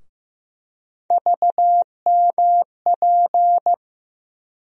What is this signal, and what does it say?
Morse code, heard as a keyed tone, about 15 wpm, 700 Hz. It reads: VMP
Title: USA Call Sign Postfixes - 3 Letters 15wpm